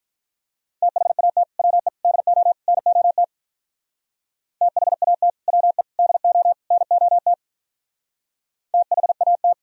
Morse code, heard as a keyed tone, about 45 wpm, 700 Hz. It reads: THAT WE DO NOT THAT WE DO NOT THAT